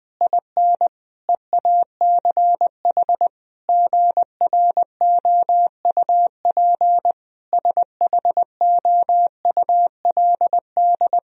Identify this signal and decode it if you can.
Morse code, heard as a keyed tone, about 20 wpm, 700 Hz. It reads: IN EACH GROUP SHOULD
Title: Sentences from Top 1000 Words - Review Part 1 - 20wpm